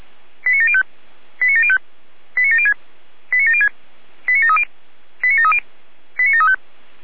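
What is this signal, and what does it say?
Five-tone selective call (ZVEI), heard as quick runs of five tones, tones of about 70 ms. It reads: 89705 89705 89706 89706 89730 89730 89735
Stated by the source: ZVEI five-tone selective call 89750 received with an SDR.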